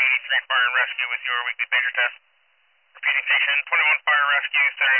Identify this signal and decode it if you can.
Emergency Alert System (SAME) tones, heard as harsh buzzing bursts, with speech. It is heard from 0.5 to 4.7 s.